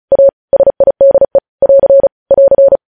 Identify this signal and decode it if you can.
Morse code, heard as a keyed tone, about 36 wpm, 550 Hz. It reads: A SIDE <AR> <AR>
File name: podcast / News headlines-1x-35x35